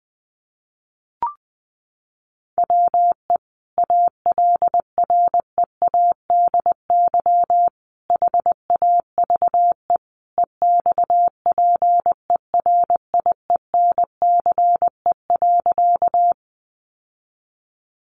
Morse code, heard as a keyed tone, about 20 wpm, 700 Hz. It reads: WE ALREADY HAVE EXPERIENCE.
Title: Sentences from Top 700 Words 20wpm